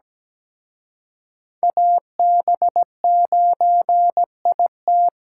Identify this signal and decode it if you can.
Morse code, heard as a keyed tone, about 20 wpm, 700 Hz. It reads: AB9IT
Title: Speed Racing - USA 2x2 Call Signs 17wpm